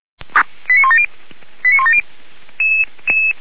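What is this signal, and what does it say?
Five-tone selective call (ZVEI), heard as quick runs of five tones, tones of about 70 ms. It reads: 89170 89170
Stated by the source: Five-tone ZVEI selective call received on SDR, recorded by feuerwehr-orlamuende.org (fire brigade Orlamünde).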